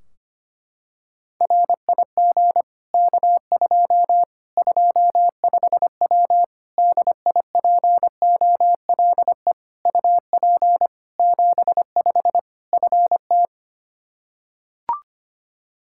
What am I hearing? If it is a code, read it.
Morse code, heard as a keyed tone, about 25 wpm, 700 Hz. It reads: RIG K2 25W DIPOLE UP 75 FT
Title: Practice QSO - Exchange 2 - Info - 25wpm